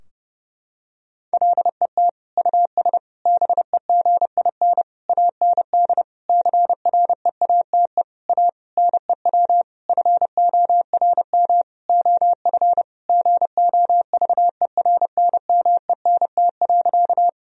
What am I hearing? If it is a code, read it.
Morse code, heard as a keyed tone, about 30 wpm, 700 Hz. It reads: LET US BEGIN AND CREATE A NEW FORM OF GOVERNMENT.
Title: Sentences from Top 1000 Words - Review Part 3 - 30wpm